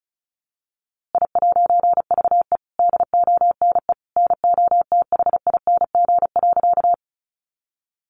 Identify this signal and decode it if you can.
Morse code, heard as a keyed tone, about 34 wpm, 700 Hz. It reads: I'VE DONE NOTHING.
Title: A Sound Of Thunder 35wpm